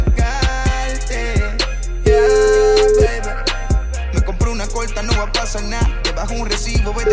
Ringback tone (Europe, 425 Hz), with noise.